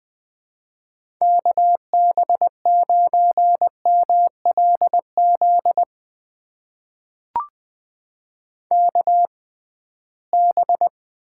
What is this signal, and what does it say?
Morse code, heard as a keyed tone, about 20 wpm, spaced as for 15 wpm, 700 Hz. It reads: KB9MLZ KB
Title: USA Call Signs - 6 Characters - 6x Character Spacing - 20wpm